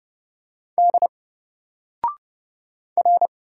Morse code, heard as a keyed tone, about 30 wpm, 700 Hz. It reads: D R
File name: Single-Letter-Number-30wpm